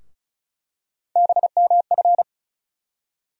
Morse code, heard as a keyed tone, about 34 wpm, 700 Hz. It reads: BMF
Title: USA Call Sign Postfixes - 3 Letters 35wpm